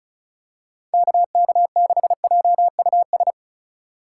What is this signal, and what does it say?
Morse code, heard as a keyed tone, about 34 wpm, 700 Hz. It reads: KK6JUS
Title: USA Call Signs - 6 Characters - 2x Character Spacing - 35wpm